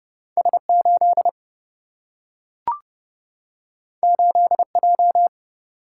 Morse code, heard as a keyed tone, about 30 wpm, 700 Hz. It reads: S 8 8 J
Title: Single-Letter Single-Number 30wpm